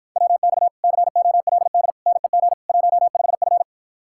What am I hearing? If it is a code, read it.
Morse code, heard as a keyed tone, about 52 wpm, 700 Hz. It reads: WX COLD NEG 15F